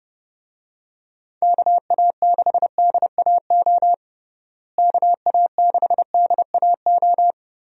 Morse code, heard as a keyed tone, about 32 wpm, 700 Hz. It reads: KA6DAO KA6DAO
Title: USA Call Signs - 6 Characters - Repeated 2x - 30wpm